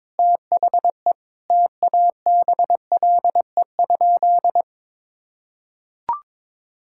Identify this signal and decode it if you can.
Morse code, heard as a keyed tone, about 22 wpm, 700 Hz. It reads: THE TABLE?
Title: Sentences from Top 1000 Words - Review Part 5 - 22wpm